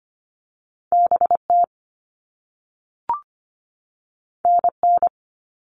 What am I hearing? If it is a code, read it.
Morse code, heard as a keyed tone, about 25 wpm, 700 Hz. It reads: B T N N
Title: All Cut Numbers - 2-Digit 25wpm